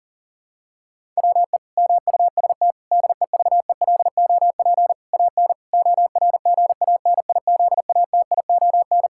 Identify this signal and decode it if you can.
Morse code, heard as a keyed tone, about 40 wpm, 700 Hz. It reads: WE MUST DEVELOP AN ORGANIZATION
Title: Sentences from Top 1000 Words - Review Part 5 - 40wpm